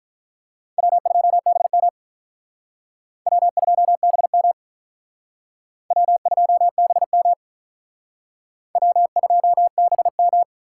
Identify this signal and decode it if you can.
Morse code, heard as a keyed tone, about 43 wpm, 700 Hz. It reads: W2BM W2BM W2BM W2BM
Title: Speed Racing - USA 1x2 Call Signs 35wpm